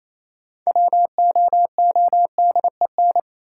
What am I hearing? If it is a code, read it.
Morse code, heard as a keyed tone, about 28 wpm, 700 Hz. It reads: WOODEN
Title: Six Letter Words - 28wpm